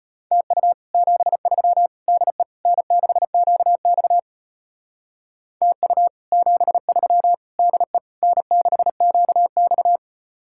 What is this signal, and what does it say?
Morse code, heard as a keyed tone, about 33 wpm, 700 Hz. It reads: TU 73 DE N6QX TU 73 DE N6QX